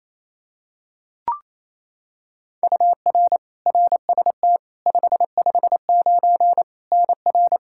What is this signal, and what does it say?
Morse code, heard as a keyed tone, about 27 wpm, 700 Hz. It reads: UR RST 559 NR
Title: Practice QSO - Exchange 1 - Beginning - 28wpm